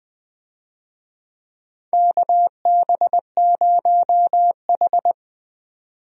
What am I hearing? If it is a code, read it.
Morse code, heard as a keyed tone, about 21 wpm, 700 Hz. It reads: KB0H
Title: USA Call Signs - 4 Characters - 20wpm